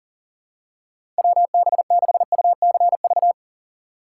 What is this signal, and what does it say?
Morse code, heard as a keyed tone, about 40 wpm, 700 Hz. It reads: WB6UCV
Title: USA Call Signs - 6 Characters - 2x Character Spacing - 40wpm